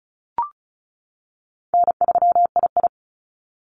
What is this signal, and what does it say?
Morse code, heard as a keyed tone, about 36 wpm, 700 Hz. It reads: N3II
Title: NA SOTA Activator Callsigns 35wpm